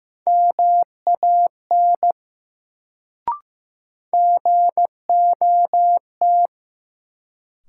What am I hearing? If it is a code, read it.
Morse code, heard as a keyed tone, about 15 wpm, 700 Hz. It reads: MAN GOT
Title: Sets of 2 Words - New Word Drill - 15wpm